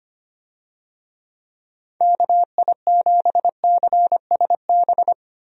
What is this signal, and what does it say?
Morse code, heard as a keyed tone, about 25 wpm, 700 Hz. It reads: KI7CSB
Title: USA Call Signs - 6 Characters - 25wpm